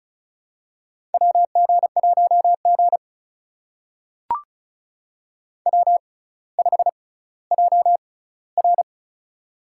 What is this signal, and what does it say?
Morse code, heard as a keyed tone, about 36 wpm, spaced as for 18 wpm, 700 Hz. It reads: WG1G W5JR
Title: USA Call Signs - 4 Characters - 6x Character Spacing - 35wpm